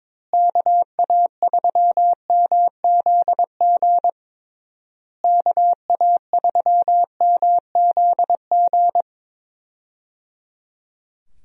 Morse code, heard as a keyed tone, about 22 wpm, 700 Hz. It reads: KA3MZG KA3MZG
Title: USA Call Signs - 6 Characters - Repeated 3x - 22wpm